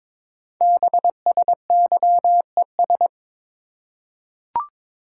Morse code, heard as a keyed tone, about 22 wpm, 700 Hz. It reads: BSYES
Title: Code Groups - 5 Letters - 22wpm